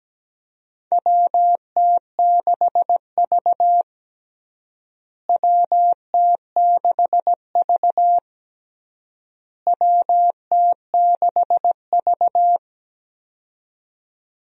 Morse code, heard as a keyed tone, about 17 wpm, 700 Hz. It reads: WT6V WT6V WT6V
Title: USA Call Signs - 4 Characters - Repeated 3x - 17wpm